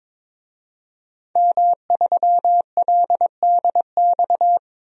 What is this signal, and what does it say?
Morse code, heard as a keyed tone, about 22 wpm, 700 Hz. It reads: M3LDX
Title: DXCC-100 Call Signs - 5 Characters - Repeated 2x - 22wpm